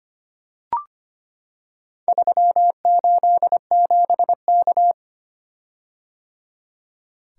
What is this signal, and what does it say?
Morse code, heard as a keyed tone, about 25 wpm, 700 Hz. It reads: 387K
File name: practice / 25wpm / Four Letter-Number Combinations - 25wpm